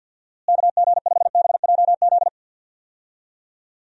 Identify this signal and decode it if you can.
Morse code, heard as a keyed tone, about 52 wpm, 700 Hz. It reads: KG5BJZ
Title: USA Call Signs - 6 Characters - Repeated 2x - 50wpm